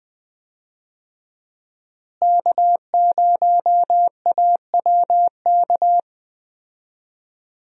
Morse code, heard as a keyed tone, about 20 wpm, 700 Hz. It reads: K0AWK
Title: USA Call Signs - 5 Characters - Repeated 3x - 20wpm